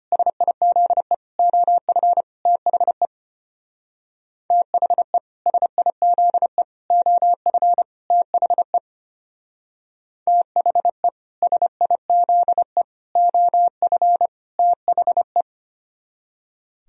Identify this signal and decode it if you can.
Morse code, heard as a keyed tone, about 33 wpm, 700 Hz. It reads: SIZE OF THE THE SIZE OF THE THE SIZE OF THE